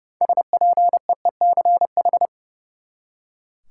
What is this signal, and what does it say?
Morse code, heard as a keyed tone, about 30 wpm, 700 Hz. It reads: SPEECH
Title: Six Letter Words - 30wpm